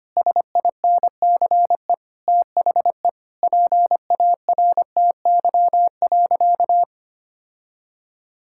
Morse code, heard as a keyed tone, about 25 wpm, 700 Hz. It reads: SINCE THE PARTY.
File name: Sentences with 2-Word Phrases - 25wpm